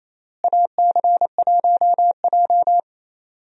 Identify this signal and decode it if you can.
Morse code, heard as a keyed tone, about 28 wpm, 700 Hz. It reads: AC1J
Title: USA Call Signs - 4 Characters - Repeated 3x - 28wpm